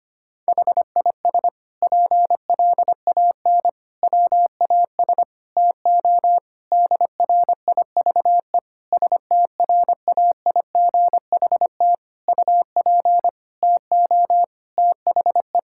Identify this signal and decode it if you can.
Morse code, heard as a keyed tone, about 25 wpm, 700 Hz. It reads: HIS PLAN WAS TO DRIVE STRAIGHT UP TO THE
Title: Sentences from Top 1000 Words - Review Part 6 - 25wpm